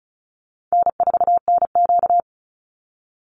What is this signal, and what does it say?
Morse code, heard as a keyed tone, about 34 wpm, 700 Hz. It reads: N4NQ
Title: Call Signs Repeated 2x - 35wpm